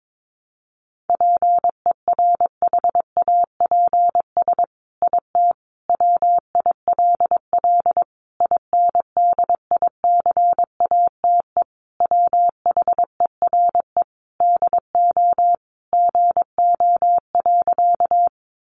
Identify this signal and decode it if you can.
Morse code, heard as a keyed tone, about 22 wpm, 700 Hz. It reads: PERHAPS IT WILL INDICATE WHERE DO GO.
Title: Sentences from Top 1300 Words 22wpm